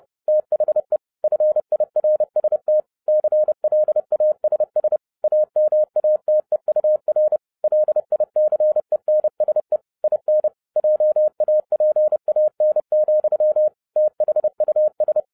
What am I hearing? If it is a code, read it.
Morse code, heard as a keyed tone, about 30 wpm, 610 Hz. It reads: THE FIRST CLASS AMATEUR LICENSE IN JAPAN, THUS
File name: 2011-07-mp3 / n40 EXCITING QRP CW OPERATION 30wpm